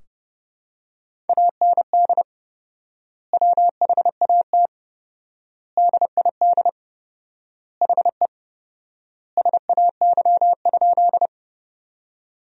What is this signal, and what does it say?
Morse code, heard as a keyed tone, about 32 wpm, 700 Hz. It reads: AND WHAT DID HE SAY?
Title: 4x Word-Spacing - Sentences from Top 200 Words - 30wpm